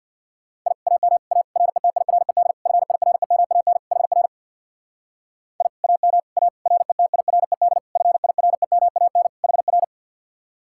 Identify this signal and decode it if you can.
Morse code, heard as a keyed tone, about 66 wpm, 700 Hz. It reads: I AM A RETIRED FIREMAN HR I AM A RETIRED FIREMAN HR